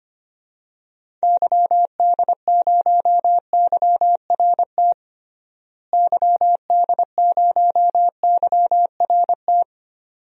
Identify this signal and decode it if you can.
Morse code, heard as a keyed tone, about 25 wpm, 700 Hz. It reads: YD0YRT YD0YRT
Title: DXCC-100 Call Signs - 6 Characters - Repeated 2x - 25wpm